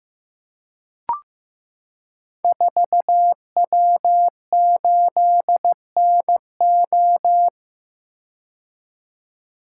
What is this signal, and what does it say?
Morse code, heard as a keyed tone, about 15 wpm, 700 Hz. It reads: 4W8NO
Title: Code Groups - 5 Letter:Numbers - 15wpm